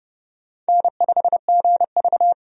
Morse code, heard as a keyed tone, about 30 wpm, 700 Hz. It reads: N5GV
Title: USA Call Signs - 4 Characters - 30wpm